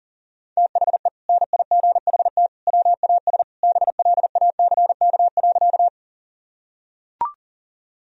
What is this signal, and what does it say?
Morse code, heard as a keyed tone, about 40 wpm, 700 Hz. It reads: THE NIGHT WAS BLACK.
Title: Sentences from Top 1000 Words - Review Part 2 - 40wpm